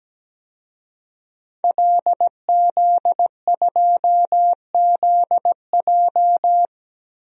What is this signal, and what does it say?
Morse code, heard as a keyed tone, about 17 wpm, 700 Hz. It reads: LZ2ZJ
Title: DXCC-100 Call Signs - 5 Characters - Repeated 2x - 17wpm